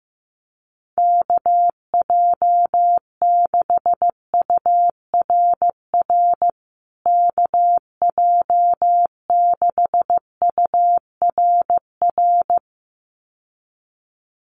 Morse code, heard as a keyed tone, about 15 wpm, 700 Hz. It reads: KJ6URR KJ6URR
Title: USA Call Signs Repeated 2x 15wpm